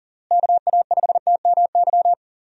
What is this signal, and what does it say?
Morse code, heard as a keyed tone, about 40 wpm, 700 Hz. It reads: KA5TMY
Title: USA Call Signs - 6 Characters - Repeated 3x - 40wpm